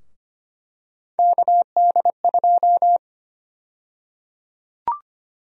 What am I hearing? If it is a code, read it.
Morse code, heard as a keyed tone, about 25 wpm, 700 Hz. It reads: KD2
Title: USA Call Sign Prefixes - Repeated 2x - 25wpm